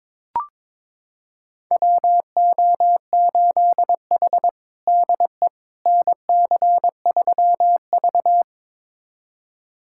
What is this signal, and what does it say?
Morse code, heard as a keyed tone, about 22 wpm, 700 Hz. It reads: WO8H DE NC3V
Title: USA Callsign DE Callsign - 4 Characters - 22wpm